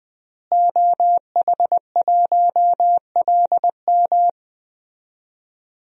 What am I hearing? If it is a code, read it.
Morse code, heard as a keyed tone, about 20 wpm, 700 Hz. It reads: OH1LM
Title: DXCC-100 Call Signs - 5 Characters - Repeated 2x - 20wpm